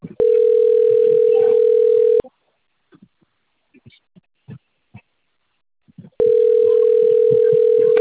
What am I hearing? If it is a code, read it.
Ringback tone (North America), 440 + 480 Hz.